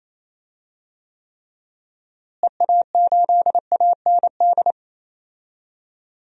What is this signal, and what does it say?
Morse code, heard as a keyed tone, about 27 wpm, 700 Hz. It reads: EA8AND
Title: DXCC-100 Call Signs - 6 Characters - Repeated 2x - 28wpm